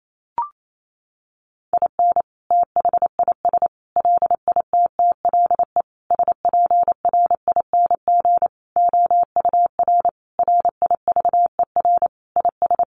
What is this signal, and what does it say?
Morse code, heard as a keyed tone, about 28 wpm, 700 Hz. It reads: IN THIS LITTLE SPRING OUR RIVER IS
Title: Sentences from Top 1200 Words 28wpm